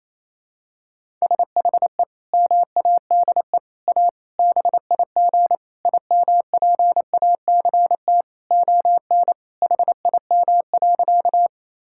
Morse code, heard as a keyed tone, about 28 wpm, 700 Hz. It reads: SHE MADE A BIG IMPACT ON HIM.
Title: Sentences from Top 1000 Words - Review Part 6 - 28wpm